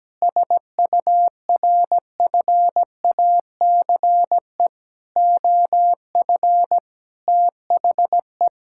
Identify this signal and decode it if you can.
Morse code, heard as a keyed tone, about 17 wpm, 700 Hz. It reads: SURFACE OF THE
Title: Sentences from Top 1000 Words - Review Part 5 - 17wpm